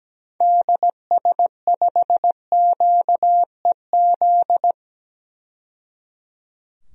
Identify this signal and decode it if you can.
Morse code, heard as a keyed tone, about 17 wpm, 700 Hz. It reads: DS5QEZ
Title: DXCC-100 Call Signs - 6 Characters - 17wpm